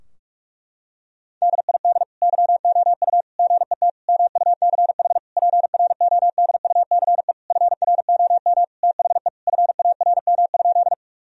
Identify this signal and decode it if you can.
Morse code, heard as a keyed tone, about 46 wpm, 700 Hz. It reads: DID YOU GET MUCH PRODUCE FROM THE FARM?
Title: Sentences from Top 1000 Words - Review Part 3 - 45wpm